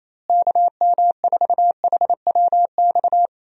Morse code, heard as a keyed tone, about 28 wpm, 700 Hz. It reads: KM4HWX
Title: USA Call Signs - 6 Characters - Repeated 2x - 28wpm